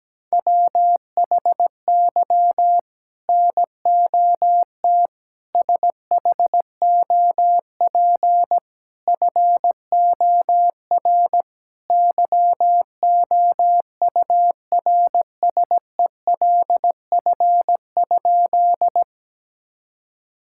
Morse code, heard as a keyed tone, about 17 wpm, 700 Hz. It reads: WHY NOT SHOP FOR YOURSELF?
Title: Sentences from Top 1600 Words 17wpm